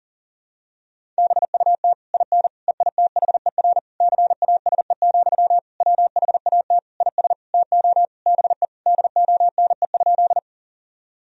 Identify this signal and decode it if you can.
Morse code, heard as a keyed tone, about 40 wpm, 700 Hz. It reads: BUT IN EITHER CASE, WHAT IS TO BE DONE?
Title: Sentences from Top 1000 Words - Review Part 4 - 40wpm